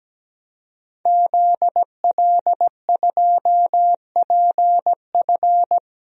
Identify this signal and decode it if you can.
Morse code, heard as a keyed tone, about 17 wpm, 700 Hz. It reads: ZL2PF
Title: DXCC-100 Call Signs - 5 Characters - 17wpm